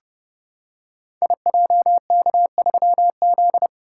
Morse code, heard as a keyed tone, about 30 wpm, 700 Hz. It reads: IJK3Z
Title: Code Groups - 5 Letter:Numbers - 30wpm